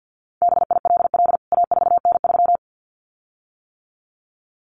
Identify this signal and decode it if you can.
Morse code, heard as a keyed tone, about 50 wpm, 700 Hz. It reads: BILL A4N3
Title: CWT Exchange 50wpm